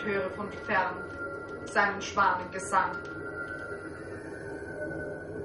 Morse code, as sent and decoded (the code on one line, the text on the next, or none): none